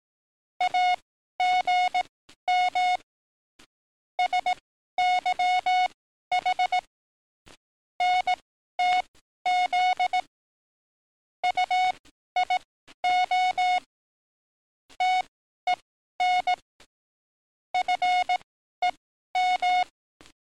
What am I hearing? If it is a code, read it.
Morse code, heard as a keyed tone, about 18 wpm, spaced as for 12 wpm, 730 Hz. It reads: AGM SYH NTZ UIO TEN FEM
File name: training / NAVPERS 11049to11060 / NAVPERS 11052 RA RB